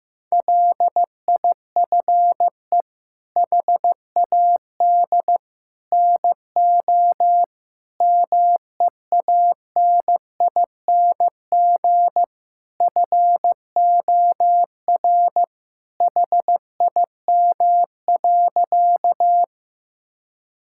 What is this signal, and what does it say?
Morse code, heard as a keyed tone, about 15 wpm, 700 Hz. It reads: LIFE HAD NO MEANING FOR HIM.
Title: Sentences from Top 1000 Words - Review Part 6 - 15wpm